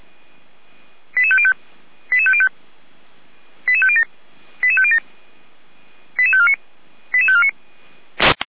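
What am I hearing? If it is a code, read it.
Five-tone selective call (ZVEI), heard as quick runs of five tones, tones of about 80 ms. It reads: 88505 88505 88507 88507 88549 88549
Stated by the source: Received by SDR as a ZVEI five-tone call.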